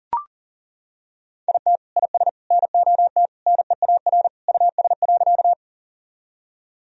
Morse code, heard as a keyed tone, about 40 wpm, 700 Hz. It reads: IT IS NOT NEAR US.
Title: Sentences from Top 1000 Words - Review Part 1 - 40wpm